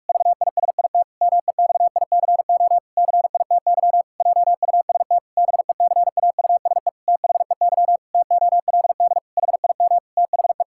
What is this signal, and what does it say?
Morse code, heard as a keyed tone, about 44 wpm, 700 Hz. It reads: VISIT MEXICO CITY JUST BECAUSE THEY TOLD HIM THE
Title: Pangram Challenge v2 - 45wpm